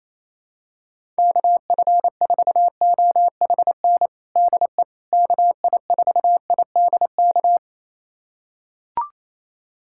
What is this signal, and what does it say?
Morse code, heard as a keyed tone, about 27 wpm, 700 Hz. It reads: KF4OHN DE KI4IDK
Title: USA Callsign DE Callsign - 6 Characters - 28wpm